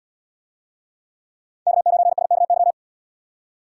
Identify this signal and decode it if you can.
Morse code, heard as a keyed tone, about 72 wpm, 700 Hz. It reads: K9SKY